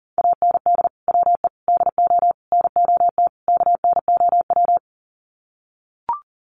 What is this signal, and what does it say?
Morse code, heard as a keyed tone, about 40 wpm, 700 Hz. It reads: AND WE DO NOT KNOW
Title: Sets of 5 Words Encore 40wpm